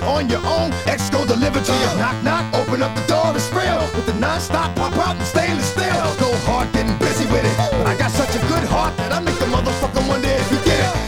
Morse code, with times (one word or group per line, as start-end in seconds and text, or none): none